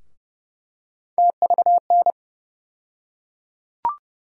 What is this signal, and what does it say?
Morse code, heard as a keyed tone, about 32 wpm, 700 Hz. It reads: TVN
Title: USA Call Sign Postfixes - 3 Letters 30wpm